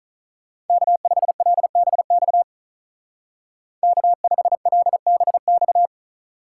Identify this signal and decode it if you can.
Morse code, heard as a keyed tone, about 44 wpm, 700 Hz. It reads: K5LBX K5LBX